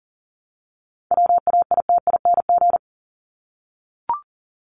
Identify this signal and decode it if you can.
Morse code, heard as a keyed tone, about 38 wpm, 700 Hz. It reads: WAITING
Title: ING Suffix - Top 100 Words with Base Words 40wpm